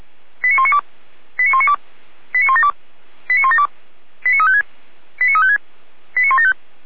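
Five-tone selective call (ZVEI), heard as quick runs of five tones, tones of about 70 ms. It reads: 89192 89192 89172 89172 89356 89356 89175
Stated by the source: ZVEI five-tone selective call received on an SDR.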